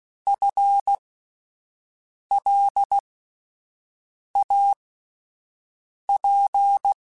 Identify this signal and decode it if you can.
Morse code, heard as a keyed tone, about 16 wpm, spaced as for 5 wpm, 800 Hz. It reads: FLAP
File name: Spaced / FL Runs sp mp3s / FL Runs sp @10